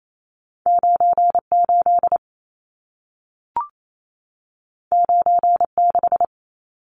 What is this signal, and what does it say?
Morse code, heard as a keyed tone, about 28 wpm, 700 Hz. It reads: 98 96
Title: Numbers - 2-Digit 28wpm